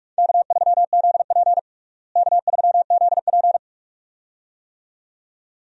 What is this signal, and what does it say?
Morse code, heard as a keyed tone, about 46 wpm, 700 Hz. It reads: K3ZP K3ZP
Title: USA Call Signs - 4 Characters - Repeated 2x - 45wpm